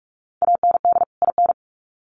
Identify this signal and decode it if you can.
Morse code, heard as a keyed tone, about 44 wpm, 700 Hz. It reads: AND IN
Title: Sets of 2 Words 45wpm